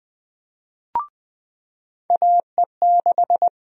Morse code, heard as a keyed tone, about 20 wpm, 700 Hz. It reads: AE6
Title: USA Call Sign Prefixes - 20wpm